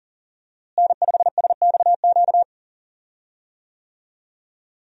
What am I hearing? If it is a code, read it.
Morse code, heard as a keyed tone, about 37 wpm, 700 Hz. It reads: N5SXQ
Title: USA Call Signs - 5 Characters - Repeated 2x - 40wpm